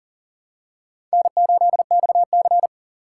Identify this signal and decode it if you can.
Morse code, heard as a keyed tone, about 40 wpm, 700 Hz. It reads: N8XC
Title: USA Call Signs - 2x Character Spacing - 40wpm